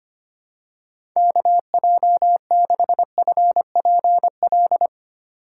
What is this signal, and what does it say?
Morse code, heard as a keyed tone, about 25 wpm, 700 Hz. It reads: KJ6FPL
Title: USA Call Signs - 3x Character Spacing - 25wpm